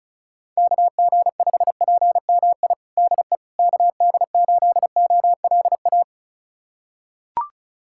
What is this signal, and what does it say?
Morse code, heard as a keyed tone, about 34 wpm, 690 Hz. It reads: KG5PMI DE KD8OLA
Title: USA Callsign DE Callsign - 6 Characters - 35wpm